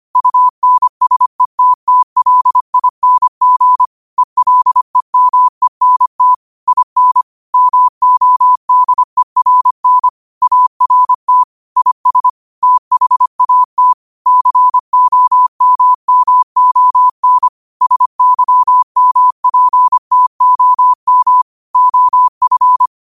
Morse code, heard as a keyed tone, about 25 wpm, 1000 Hz. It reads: UNSETTLING ELEMENT IN MODERN ART IS THAT COMMON SYMPTOM OF